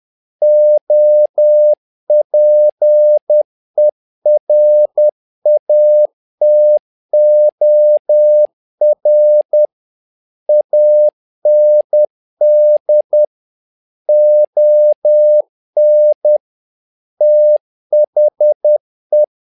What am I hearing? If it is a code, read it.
Morse code, heard as a keyed tone, about 10 wpm, 600 Hz. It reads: OPERATOR AND ON THE
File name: learning center / tower trouble / tower trouble 10wpm